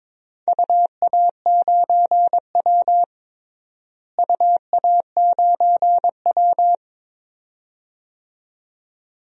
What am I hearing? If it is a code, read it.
Morse code, heard as a keyed tone, about 22 wpm, 700 Hz. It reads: UA9W UA9W
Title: DXCC-100 Call Signs - 4 Characters - Repeated 2x - 22wpm